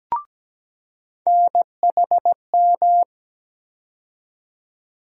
Morse code, heard as a keyed tone, about 17 wpm, 700 Hz. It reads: NHM
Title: USA Call Sign Postfixes - 3 Letters 17wpm